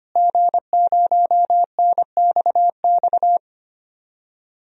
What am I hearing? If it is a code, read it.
Morse code, heard as a keyed tone, about 25 wpm, 700 Hz. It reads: G0NXX
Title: DXCC-100 Call Signs - 5 Characters - Repeated 2x - 25wpm